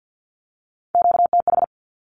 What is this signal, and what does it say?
Morse code, heard as a keyed tone, about 48 wpm, 700 Hz. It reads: QTH
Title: Q-Codes Repeated 3x 50wpm